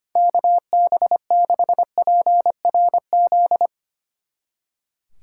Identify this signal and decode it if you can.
Morse code, heard as a keyed tone, about 33 wpm, 700 Hz. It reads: KB6PRZ